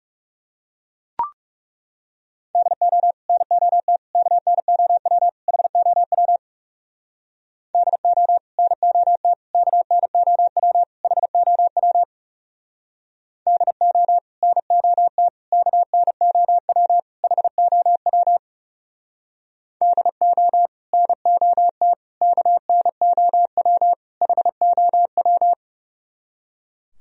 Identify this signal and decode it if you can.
Morse code, heard as a keyed tone, about 36 wpm, 700 Hz. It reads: DO NOT KNOW HOW DO NOT KNOW HOW DO NOT KNOW HOW DO NOT KNOW HOW
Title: Speed-Racing - Sets of 4 Words 30wpm